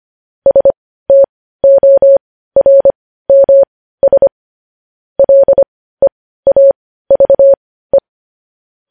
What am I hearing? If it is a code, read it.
Morse code, heard as a keyed tone, about 25 wpm, spaced as for 15 wpm, 550 Hz. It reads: STORMS LEAVE
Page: news headlines in morse code at 15 wpm